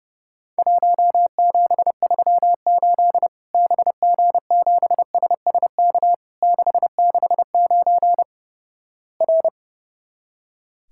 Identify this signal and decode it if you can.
Morse code, heard as a keyed tone, about 31 wpm, 700 Hz. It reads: 1738 BG7SSK 669 R
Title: Worked All Europe DX Contest - QTC Exchange - 30wpm